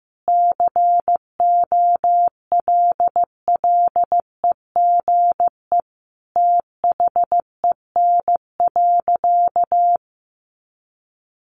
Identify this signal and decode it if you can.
Morse code, heard as a keyed tone, about 15 wpm, 700 Hz. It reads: COLLEGE THEN.
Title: Sentences from Top 1300 Words 15wpm